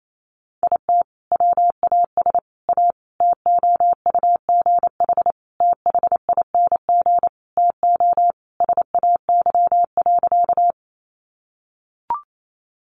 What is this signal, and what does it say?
Morse code, heard as a keyed tone, about 28 wpm, 700 Hz. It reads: IT WAS A TOUGH THING TO SAY.
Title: Sentences from Top 1200 Words 28wpm